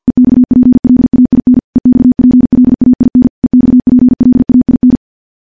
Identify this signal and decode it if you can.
Morse code, heard as a keyed tone, about 50 wpm, 250 Hz. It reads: EXPLAIN EXPLAIN EXPLAIN
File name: morse / morse-training-50WPM